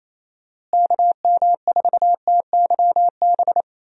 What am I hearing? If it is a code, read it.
Morse code, heard as a keyed tone, about 28 wpm, 700 Hz. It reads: KM4TYB
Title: USA Call Signs - 6 Characters - Repeated 2x - 28wpm